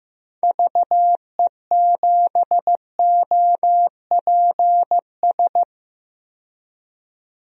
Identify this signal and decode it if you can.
Morse code, heard as a keyed tone, about 15 wpm, 700 Hz. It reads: VE7OPS
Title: DXCC-100 Call Signs - 6 Characters - 15wpm